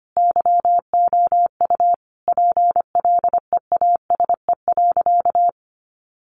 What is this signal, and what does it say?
Morse code, heard as a keyed tone, about 25 wpm, 700 Hz. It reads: YOU PLEASE.
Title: Sentences from Top 1500 Words 25wpm